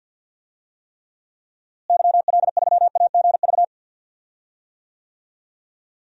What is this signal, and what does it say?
Morse code, heard as a keyed tone, about 51 wpm, 700 Hz. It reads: YL3AGV
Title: DXCC-100 Call Signs - 6 Characters - Repeated 2x - 50wpm